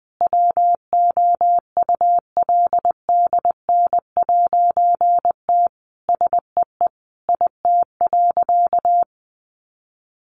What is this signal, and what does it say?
Morse code, heard as a keyed tone, about 20 wpm, 700 Hz. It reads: WOULDN'T SEE IT.
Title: A Sound Of Thunder 20wpm